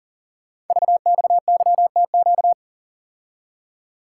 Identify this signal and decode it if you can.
Morse code, heard as a keyed tone, about 38 wpm, 690 Hz. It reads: VXYTQ
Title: Code Groups - 5 Letter:Numbers - 40wpm